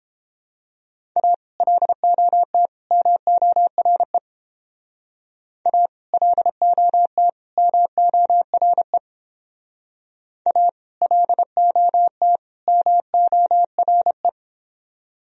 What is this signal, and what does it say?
Morse code, heard as a keyed tone, about 31 wpm, 700 Hz. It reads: A LOT MORE A LOT MORE A LOT MORE